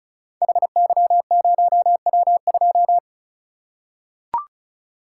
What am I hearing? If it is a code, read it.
Morse code, heard as a keyed tone, about 36 wpm, 700 Hz. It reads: HY0W2
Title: Code Groups - 5 Letter:Numbers - 35wpm